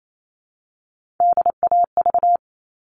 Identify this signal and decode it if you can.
Morse code, heard as a keyed tone, about 28 wpm, 700 Hz. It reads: DAV
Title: All Cut Numbers - 3-Digit 28wpm